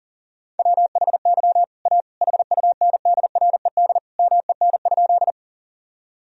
Morse code, heard as a keyed tone, about 37 wpm, 700 Hz. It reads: WHY A HUNDRED MEN?
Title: Sentences from Top 1000 Words - Review Part 2 - 40wpm